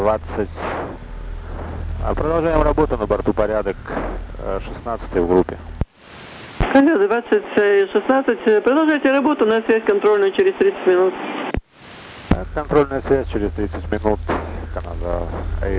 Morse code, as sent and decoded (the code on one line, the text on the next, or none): none